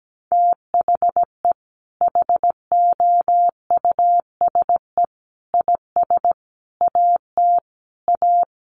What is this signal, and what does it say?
Morse code, heard as a keyed tone, about 17 wpm, 700 Hz. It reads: THE HOUSE IS AT A
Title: Sentences from Top 1500 Words 17wpm